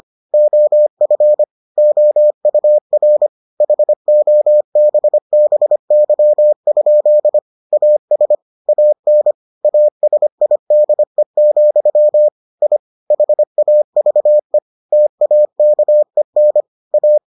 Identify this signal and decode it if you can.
Morse code, heard as a keyed tone, about 25 wpm, 600 Hz. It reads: OF OUR HOBBY? AS AN ASIDE, I HAVE TAKEN A